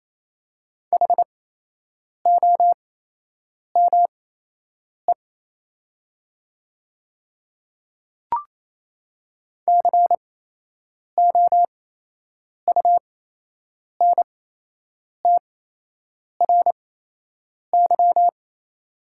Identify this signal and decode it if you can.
Morse code, heard as a keyed tone, about 28 wpm, spaced as for 8 wpm, 700 Hz. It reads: HOME COUNTRY
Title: 8x Character Spacing - Top 300 Words - 28wpm